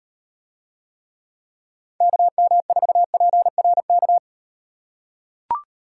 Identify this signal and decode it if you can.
Morse code, heard as a keyed tone, about 38 wpm, 700 Hz. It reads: KM4PRK